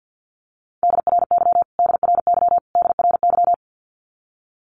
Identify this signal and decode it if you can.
Morse code, heard as a keyed tone, about 50 wpm, 700 Hz. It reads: DRY DRY DRY
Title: Top 500 Words Repeated 50wpm